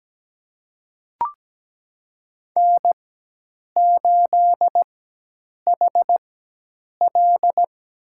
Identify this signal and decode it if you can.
Morse code, heard as a keyed tone, about 17 wpm, spaced as for 8 wpm, 700 Hz. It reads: N8HL
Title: USA Call Signs - 4x Character Spacing - 17wpm